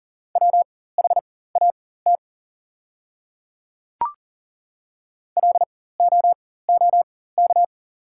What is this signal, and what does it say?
Morse code, heard as a keyed tone, about 38 wpm, spaced as for 18 wpm, 690 Hz. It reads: WHAT LOOK
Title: 4x Character Spacing - Top 200 Words - 40wpm